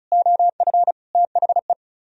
Morse code, heard as a keyed tone, about 43 wpm, 700 Hz. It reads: OF THE
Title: Speed-Racing - Sets of 4 Words 35wpm